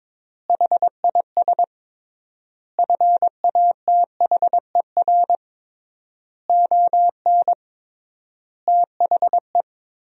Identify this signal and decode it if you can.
Morse code, heard as a keyed tone, about 22 wpm, 700 Hz. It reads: HIS FATHER ON THE
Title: 3x Word-Spacing - Sentences from Top 500 Words - 22wpm